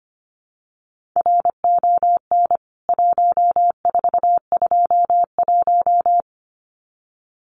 Morse code, heard as a keyed tone, about 25 wpm, 700 Hz. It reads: RON 1421
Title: CWT Exchange 25wpm